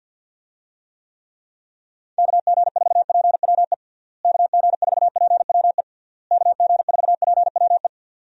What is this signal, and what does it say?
Morse code, heard as a keyed tone, about 50 wpm, 700 Hz. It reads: KG4PWE KG4PWE KG4PWE
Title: USA Call Signs - 6 Characters - Repeated 3x - 50wpm